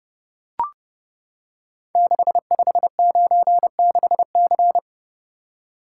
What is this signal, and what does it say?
Morse code, heard as a keyed tone, about 30 wpm, 700 Hz. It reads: 6596C
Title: Code Groups - 5 Letter:Numbers - 30wpm